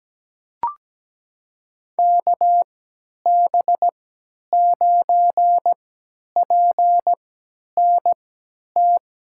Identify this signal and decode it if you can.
Morse code, heard as a keyed tone, about 17 wpm, spaced as for 10 wpm, 700 Hz. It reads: KB9PNT
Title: USA Call Signs - 6 Characters - 3x Character Spacing - 17wpm